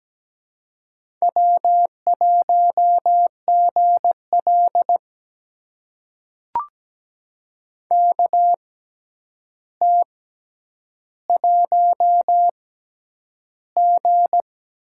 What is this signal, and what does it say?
Morse code, heard as a keyed tone, about 17 wpm, spaced as for 9 wpm, 700 Hz. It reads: W1GL KT1G
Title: USA Call Signs - 4 Characters - 6x Character Spacing - 17wpm